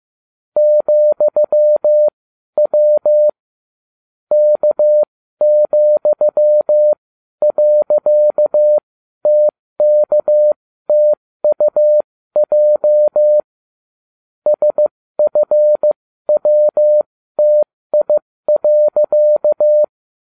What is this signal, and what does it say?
Morse code, heard as a keyed tone, about 15 wpm, spaced as for 12 wpm, 600 Hz. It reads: ,W K,.TKTUJ SFWTI.